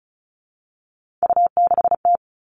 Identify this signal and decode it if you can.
Morse code, heard as a keyed tone, about 36 wpm, 700 Hz. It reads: U6T
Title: Three Challenging Letter-Number Combinations 35wpm